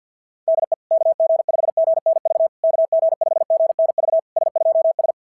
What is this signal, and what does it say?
Morse code, heard as a keyed tone, about 46 wpm, 660 Hz. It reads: DE KG5GNV KG5GNV S2S
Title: SOTA QSO Practice 50wpm